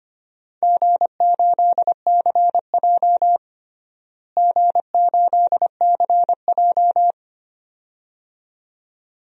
Morse code, heard as a keyed tone, about 25 wpm, 700 Hz. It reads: G8CJ G8CJ
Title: DXCC-100 Call Signs - 4 Characters - Repeated 2x - 25wpm